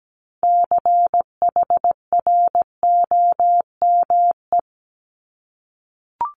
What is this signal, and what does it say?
Morse code, heard as a keyed tone, about 17 wpm, 700 Hz. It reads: CHROME
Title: Multi-letter Phonemes Repeated 3x 17wpm